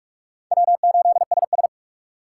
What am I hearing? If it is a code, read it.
Morse code, heard as a keyed tone, about 41 wpm, 700 Hz. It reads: W8SS
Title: USA Call Signs - 4 Characters - 4x Character Spacing - 45wpm